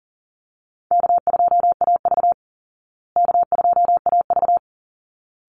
Morse code, heard as a keyed tone, about 40 wpm, 700 Hz. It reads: K2AV K2AV
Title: Easy Call Signs Repeated 2x - 40wpm